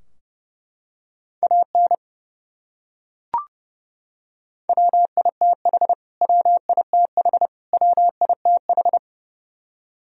Morse code, heard as a keyed tone, about 32 wpm, 700 Hz. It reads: AN WITH WITH WITH
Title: Top 100 Words Repeated 30wpm